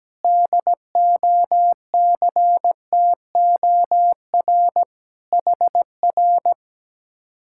Morse code, heard as a keyed tone, about 17 wpm, 700 Hz. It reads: DOCTOR HR
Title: Speed-Racing Common QSO Phrases - 17wpm